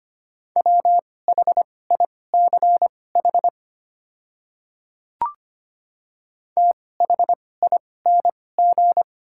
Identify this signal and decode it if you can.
Morse code, heard as a keyed tone, about 25 wpm, spaced as for 18 wpm, 700 Hz. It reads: WHICH THING
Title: 2x Character Spacing - Top 100 Words - 25wpm